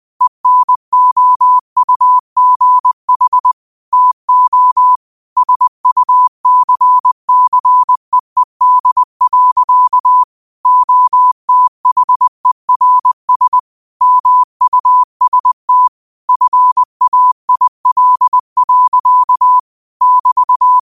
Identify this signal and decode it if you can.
Morse code, heard as a keyed tone, about 20 wpm, 1000 Hz. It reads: ENOUGH TO SUCCEED. OTHERS MUST FAIL. <BT>
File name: qotdfiles / 20wpm